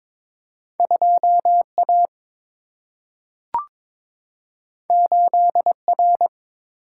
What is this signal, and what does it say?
Morse code, heard as a keyed tone, about 22 wpm, 700 Hz. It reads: 2A 8R
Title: Single-Letter Single-Number 22wpm